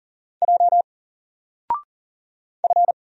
Morse code, heard as a keyed tone, about 40 wpm, 700 Hz. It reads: J F
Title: Single Character (Letter:Number:Punctuation:Prosign) 40wpm